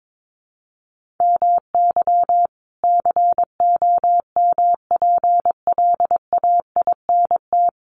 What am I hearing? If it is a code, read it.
Morse code, heard as a keyed tone, about 22 wpm, 700 Hz. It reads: MY COMPLAINT
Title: Sentences from Top 1400 Words 22wpm